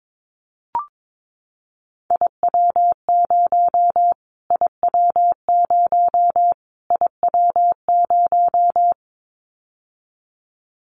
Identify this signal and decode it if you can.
Morse code, heard as a keyed tone, about 22 wpm, 700 Hz. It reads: IW0 IW0 IW0
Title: Callsign Prefixes for Top 25 DXCC - Repeated 3x - 22wpm